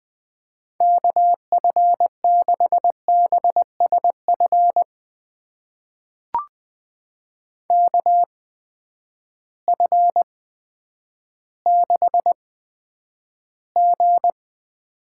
Morse code, heard as a keyed tone, about 20 wpm, spaced as for 10 wpm, 700 Hz. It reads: KF6BSF KF6G
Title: USA Call Signs - 6 Characters - 8x Character Spacing - 20wpm